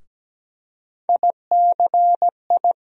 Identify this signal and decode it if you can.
Morse code, heard as a keyed tone, about 17 wpm, 700 Hz. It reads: ICI
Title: USA Call Sign Postfixes - 3 Letters 17wpm